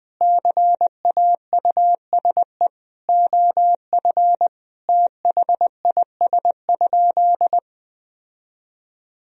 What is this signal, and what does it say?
Morse code, heard as a keyed tone, about 20 wpm, 700 Hz. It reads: CAUSE OF THIS?
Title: Sentences from Top 1000 Words - Review Part 1 - 20wpm